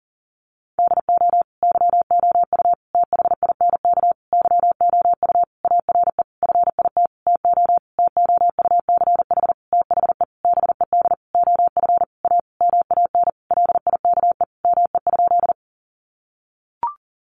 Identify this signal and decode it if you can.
Morse code, heard as a keyed tone, about 40 wpm, 700 Hz. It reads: DO YOU THINK YOU ARE FIT TO TOUCH THE BED OF A MAN LIKE ME?
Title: Sentences from Top 1200 Words 40wpm